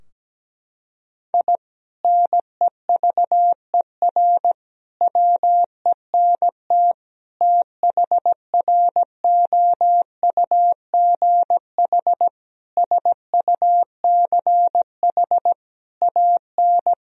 Morse code, heard as a keyed tone, about 17 wpm, 700 Hz. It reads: I NEVER WENT THROUGH SUCH AN
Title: Sentences from Top 1000 Words - Review Part 3 - 17wpm